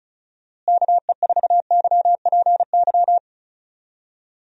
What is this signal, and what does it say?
Morse code, heard as a keyed tone, about 34 wpm, 700 Hz. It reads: KE4YPY
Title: USA Call Signs - 6 Characters - Repeated 3x - 35wpm